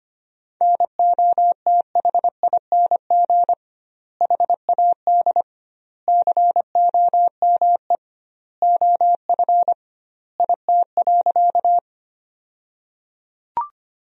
Morse code, heard as a keyed tone, about 25 wpm, 700 Hz. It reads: NOTHING HAD COME OF IT.
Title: 2x Word-Spacing - Sentences from Top 500 Words - 25wpm